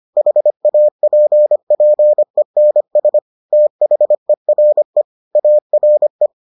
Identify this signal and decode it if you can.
Morse code, heard as a keyed tone, about 25 wpm, 600 Hz. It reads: HAPPENS THERE ARE